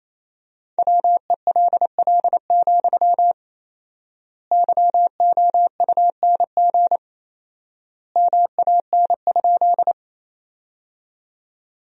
Morse code, heard as a keyed tone, about 28 wpm, 700 Hz. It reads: WELL, YOUNG MAN?
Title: 4x Word-Spacing - Sentences from Top 400 Words - 28wpm